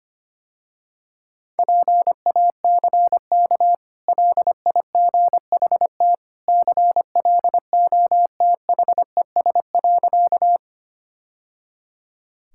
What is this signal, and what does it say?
Morse code, heard as a keyed tone, about 25 wpm, 690 Hz. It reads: PACK LIGHT CLOTHES.
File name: Easy Sentences with Hints - 25wpm